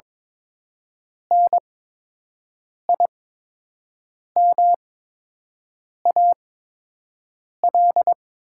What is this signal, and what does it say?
Morse code, heard as a keyed tone, about 22 wpm, spaced as for 6 wpm, 700 Hz. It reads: NIMAL
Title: 8x Character Spacing - Top 200 Words - 22wpm